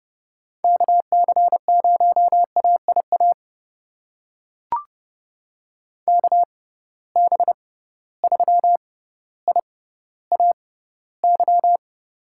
Morse code, heard as a keyed tone, about 30 wpm, spaced as for 15 wpm, 700 Hz. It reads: KC0AIA KB3IAY
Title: USA Call Signs - 6 Characters - 6x Character Spacing - 30wpm